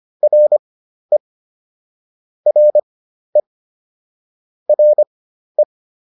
Morse code, heard as a keyed tone, about 25 wpm, spaced as for 12 wpm, 600 Hz. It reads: RE RE RE